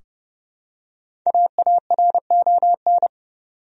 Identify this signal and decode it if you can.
Morse code, heard as a keyed tone, about 30 wpm, 700 Hz. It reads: AARON
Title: USA Common Names - 30wpm